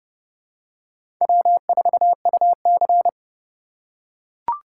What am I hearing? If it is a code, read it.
Morse code, heard as a keyed tone, about 31 wpm, 700 Hz. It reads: W4UC
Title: USA Call Signs - 4 Characters - 6x Character Spacing - 30wpm